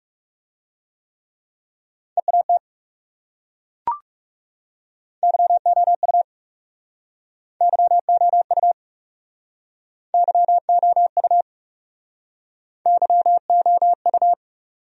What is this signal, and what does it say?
Morse code, heard as a keyed tone, about 38 wpm, 700 Hz. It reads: EAT YOU YOU YOU YOU